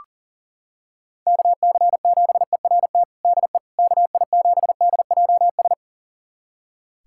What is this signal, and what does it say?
Morse code, heard as a keyed tone, about 37 wpm, 700 Hz. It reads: KC7ERT DE KI7DJS
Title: USA Callsign DE Callsign - 6 Characters - 40wpm